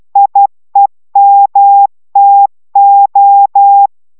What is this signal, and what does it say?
Morse code, heard as a keyed tone, about 12 wpm, 800 Hz. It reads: IEMTO